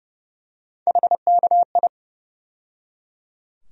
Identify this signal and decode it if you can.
Morse code, heard as a keyed tone, about 30 wpm, 700 Hz. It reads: HKI
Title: USA Call Sign Postfixes - 3 Letters 30wpm